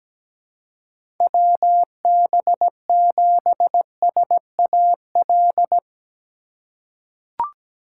Morse code, heard as a keyed tone, about 17 wpm, 700 Hz. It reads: WB7SAL
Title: USA Call Signs - 6 Characters - 4x Character Spacing - 17wpm